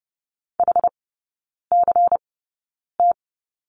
Morse code, heard as a keyed tone, about 30 wpm, spaced as for 9 wpm, 700 Hz. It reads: HCT
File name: Lesson 16e - Letter c - Three Characters 30wpm